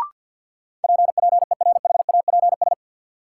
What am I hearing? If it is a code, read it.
Morse code, heard as a keyed tone, about 53 wpm, 700 Hz. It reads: PPERHAPS
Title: Sentences from Top 1100 Words - New Word Drill Alt - 50wpm